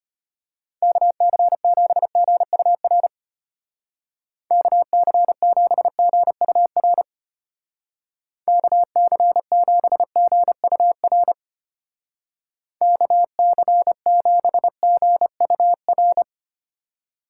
Morse code, heard as a keyed tone, about 32 wpm, 700 Hz. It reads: KC7GUR KC7GUR KC7GUR KC7GUR